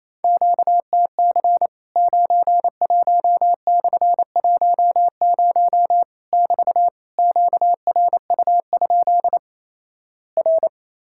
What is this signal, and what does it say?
Morse code, heard as a keyed tone, about 28 wpm, 700 Hz. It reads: QTC 91/10 <BT> QRU? R
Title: Worked All Europe DX Contest - QTC Exchange - 28wpm